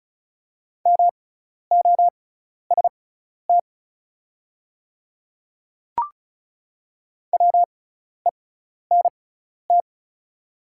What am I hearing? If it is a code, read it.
Morse code, heard as a keyed tone, about 34 wpm, spaced as for 12 wpm, 700 Hz. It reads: MOST WENT
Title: 6x Character Spacing - Top 200 Words - 35wpm